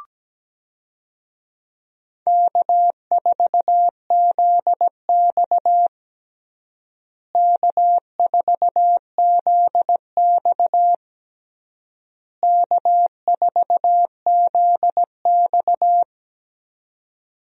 Morse code, heard as a keyed tone, about 17 wpm, 700 Hz. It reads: K4ZX K4ZX K4ZX
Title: USA Call Signs - 4 Characters - Repeated 3x - 17wpm